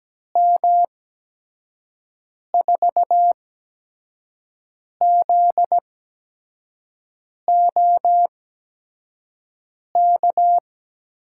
Morse code, heard as a keyed tone, about 17 wpm, spaced as for 5 wpm, 700 Hz. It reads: M4ZOK
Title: USA Call Signs - 8x Character Spacing - 17wpm